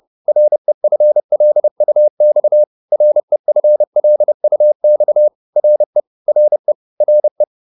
Morse code, heard as a keyed tone, about 30 wpm, 600 Hz. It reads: REFLUX REFLUX RE RE RE